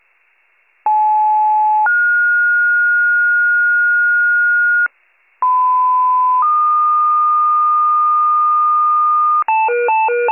Two-tone page, heard as a one-second tone, then a three-second tone, about 860 then 1430 Hz, about 1010 then 1220 Hz, with noise. It is heard from 0.9 to 9.4 s.